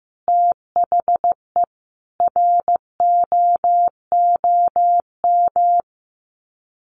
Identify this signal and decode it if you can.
Morse code, heard as a keyed tone, about 15 wpm, 700 Hz. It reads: THE ROOM
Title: Sets of 5 Words Encore2 15wpm